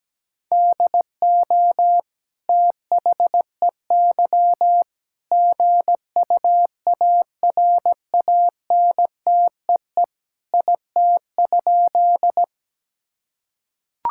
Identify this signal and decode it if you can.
Morse code, heard as a keyed tone, about 17 wpm, 700 Hz. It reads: DO THEY GUARANTEE IT?
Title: Sentences from Top 1600 Words 17wpm